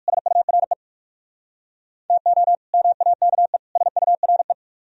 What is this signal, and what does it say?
Morse code, heard as a keyed tone, about 66 wpm, spaced as for 56 wpm, 700 Hz. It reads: SURE TO MAKE SURE
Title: Speed-Racing - Sets of 3 Words 45wpm